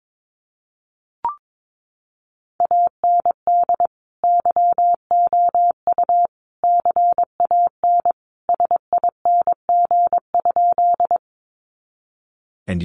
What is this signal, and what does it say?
Morse code, heard as a keyed tone, about 22 wpm, 700 Hz. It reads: AND YOU CAN SING?
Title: Sentences from Top 400 Words 22wpm